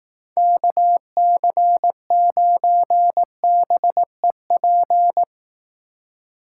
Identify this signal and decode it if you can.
Morse code, heard as a keyed tone, about 18 wpm, 700 Hz. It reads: KC9BEP
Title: Speed Racing - USA Call Signs 15wpm